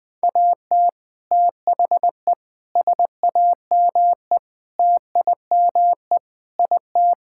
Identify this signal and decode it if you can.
Morse code, heard as a keyed tone, about 21 wpm, 700 Hz. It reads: AT THE SAME TIME IT
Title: Sentences with 3-Word Phrases - Spoken Beginning - 20wpm